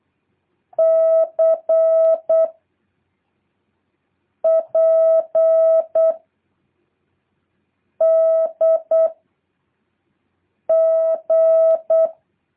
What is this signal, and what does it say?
Morse code, heard as a keyed tone, about 8 wpm, 650 Hz. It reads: C P D G
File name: jh8cbh / 2023cw